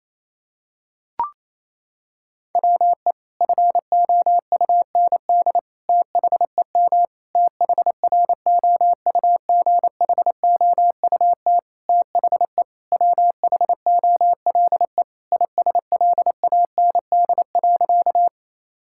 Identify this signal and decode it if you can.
Morse code, heard as a keyed tone, about 28 wpm, 700 Hz. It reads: WE FOUND THEM THROUGHOUT THE WHOLE ISLAND.
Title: Sentences from Top 1000 Words - Review Part 5 - 28wpm